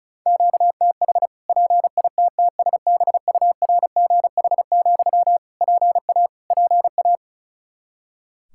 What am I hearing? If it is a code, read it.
Morse code, heard as a keyed tone, about 36 wpm, 700 Hz. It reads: QTH PITTSBURGH, PA PA
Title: Practice QSO - Exchange 1 - Beginning - 35wpm